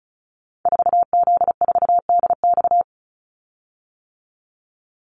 Morse code, heard as a keyed tone, about 34 wpm, 700 Hz. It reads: 4Z4DX
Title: Call Signs - 35wpm